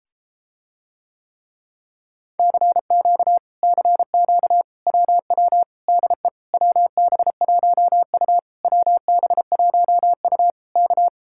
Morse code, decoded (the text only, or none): CQ CQ WW DE WB1U WB1U K